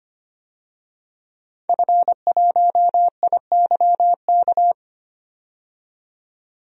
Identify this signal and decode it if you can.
Morse code, heard as a keyed tone, about 25 wpm, 700 Hz. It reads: F1IYK
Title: DXCC-100 Call Signs - 5 Characters - 25wpm